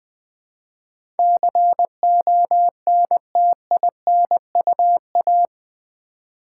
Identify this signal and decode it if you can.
Morse code, heard as a keyed tone, about 20 wpm, 700 Hz. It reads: CONTINUA
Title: Build A Word - 4 Letters At A Time - 11-Letter Words - 20wpm